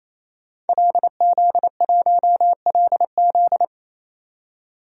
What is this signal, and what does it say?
Morse code, heard as a keyed tone, about 28 wpm, 700 Hz. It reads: LZ1LZ
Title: DXCC-100 Call Signs - 5 Characters - 28wpm